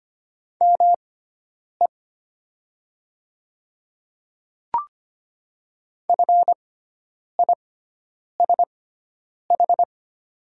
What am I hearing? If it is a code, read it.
Morse code, heard as a keyed tone, about 25 wpm, spaced as for 9 wpm, 700 Hz. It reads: ME FISH
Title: 6x Character Spacing - Top 300 Words - 25wpm